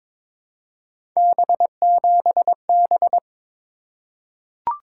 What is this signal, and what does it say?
Morse code, heard as a keyed tone, about 22 wpm, 700 Hz. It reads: B7B
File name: practice / 22wpm / Three Letter-Number Combinations - 22wpm